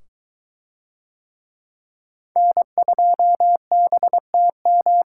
Morse code, heard as a keyed tone, about 20 wpm, 700 Hz. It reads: N2BTM